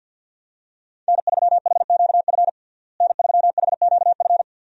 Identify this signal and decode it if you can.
Morse code, heard as a keyed tone, about 51 wpm, 700 Hz. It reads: N3HQF N3HQF
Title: USA Call Signs - 5 Characters - Repeated 2x - 50wpm